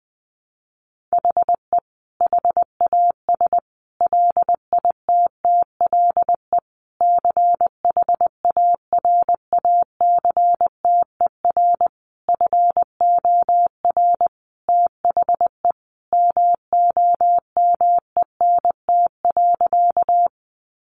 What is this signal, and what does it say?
Morse code, heard as a keyed tone, about 20 wpm, 700 Hz. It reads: HE HAS LITTLE CHARACTER FOR THE MOMENT.
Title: Sentences from Top 1200 Words 20wpm